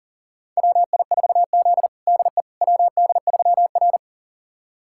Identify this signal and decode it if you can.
Morse code, heard as a keyed tone, about 40 wpm, 700 Hz. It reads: WI4Z DE WD3R
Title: USA Callsign DE Callsign - 4 Characters - 40wpm